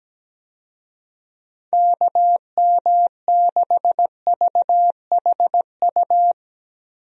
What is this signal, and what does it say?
Morse code, heard as a keyed tone, about 17 wpm, 700 Hz. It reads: KM6VHU
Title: USA Call Signs - 6 Characters - Repeated 3x - 17wpm